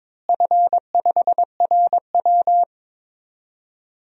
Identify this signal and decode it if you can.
Morse code, heard as a keyed tone, about 22 wpm, 700 Hz. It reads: F5RW
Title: DXCC-100 Call Signs - 4 Characters - Repeated 3x - 22wpm